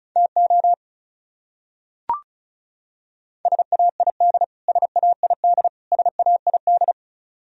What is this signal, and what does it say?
Morse code, heard as a keyed tone, about 36 wpm, 700 Hz. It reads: TO SAID SAID SAID
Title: Top 100 Words Repeated 35wpm